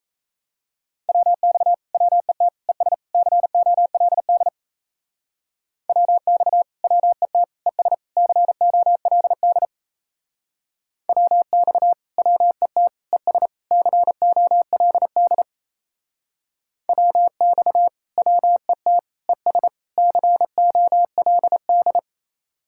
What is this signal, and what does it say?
Morse code, heard as a keyed tone, about 35 wpm, 700 Hz. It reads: WX WET ES COLD WX WET ES COLD WX WET ES COLD WX WET ES COLD